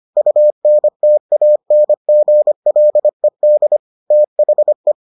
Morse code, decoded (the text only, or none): UNTANGLED THE